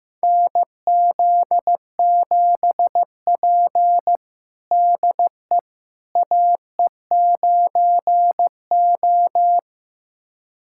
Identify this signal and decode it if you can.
Morse code, heard as a keyed tone, about 15 wpm, 700 Hz. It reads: NZ7P DE AE9O
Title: USA Callsign DE Callsign - 4 Characters - 15wpm